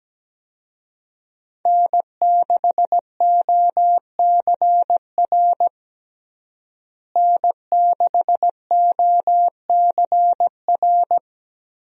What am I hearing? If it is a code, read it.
Morse code, heard as a keyed tone, about 17 wpm, 700 Hz. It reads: N6OCR N6OCR
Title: USA Call Signs - 5 Characters - Repeated 2x - 17wpm